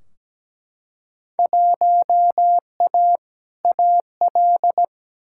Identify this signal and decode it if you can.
Morse code, heard as a keyed tone, about 17 wpm, 700 Hz. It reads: 1A AL
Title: Field Day Exchanges - 17wpm